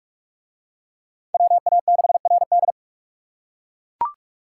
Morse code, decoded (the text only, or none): WA6RD